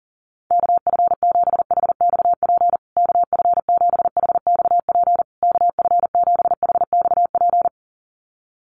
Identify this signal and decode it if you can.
Morse code, heard as a keyed tone, about 40 wpm, 700 Hz. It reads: KF7HXP KF7HXP KF7HXP
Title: USA Call Signs Repeated 3x 40wpm